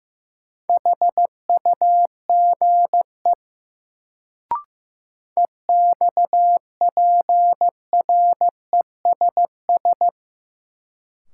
Morse code, heard as a keyed tone, about 15 wpm, 700 Hz. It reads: HUGE EXPRESS
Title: Sentences from Top 1100 Words - New Word Drill - 15wpm